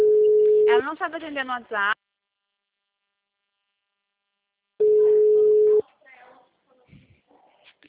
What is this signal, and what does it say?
Ringback tone (Europe, 425 Hz), with speech.